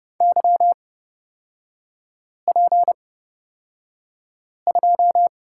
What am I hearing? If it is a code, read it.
Morse code, heard as a keyed tone, about 29 wpm, spaced as for 5 wpm, 700 Hz. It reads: YP2
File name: Lesson 38e - Letter q - Three Characters 30wpm